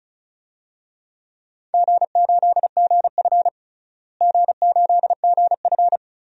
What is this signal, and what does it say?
Morse code, heard as a keyed tone, about 35 wpm, 700 Hz. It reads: G8GF G8GF
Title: DXCC-100 Call Signs - 4 Characters - Repeated 2x - 35wpm